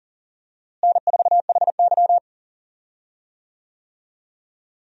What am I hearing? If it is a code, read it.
Morse code, heard as a keyed tone, about 40 wpm, 700 Hz. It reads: N4HY
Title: USA Call Signs - 4 Characters - Repeated 2x - 40wpm